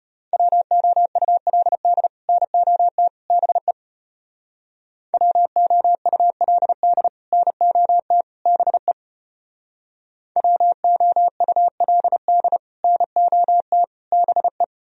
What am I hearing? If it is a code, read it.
Morse code, heard as a keyed tone, about 33 wpm, 700 Hz. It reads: WOULD NOT BE WOULD NOT BE WOULD NOT BE